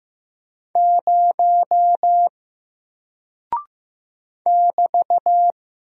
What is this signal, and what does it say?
Morse code, heard as a keyed tone, about 15 wpm, 700 Hz. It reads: 0 <BT>
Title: Single Character (Letter:Number:Punctuation:Prosign) Alt 15wpm